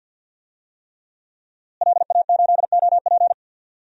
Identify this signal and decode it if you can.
Morse code, heard as a keyed tone, about 47 wpm, 700 Hz. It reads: LA8OP
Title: DXCC-100 Call Signs - 5 Characters - Repeated 3x - 50wpm